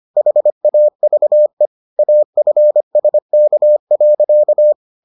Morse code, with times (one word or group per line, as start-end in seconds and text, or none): 0.2-1.7 HAVE
2.0-4.7 AFSK.